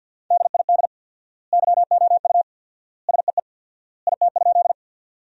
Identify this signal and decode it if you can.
Morse code, heard as a keyed tone, about 50 wpm, 700 Hz. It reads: DID YOU SEE IT?
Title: 4x Word-Spacing - Sentences from Top 100 Words - 50wpm